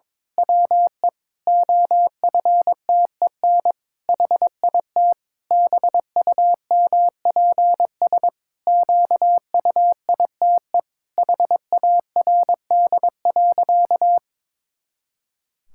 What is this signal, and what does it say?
Morse code, heard as a keyed tone, about 22 wpm, 700 Hz. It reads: WE OFTEN HIT BUMPS QUITE HARD.
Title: Pangram Challenge v2 - 22wpm